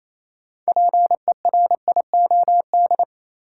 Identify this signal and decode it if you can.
Morse code, heard as a keyed tone, about 28 wpm, 700 Hz. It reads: PERIOD
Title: Sets of 5 Words - New Word Drill - 28wpm